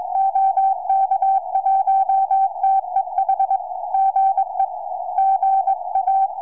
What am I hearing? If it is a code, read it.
Morse code, heard as a keyed tone, about 20 wpm, 770 Hz. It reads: OK1TEH GE GA
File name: tropo / ua2fl 2m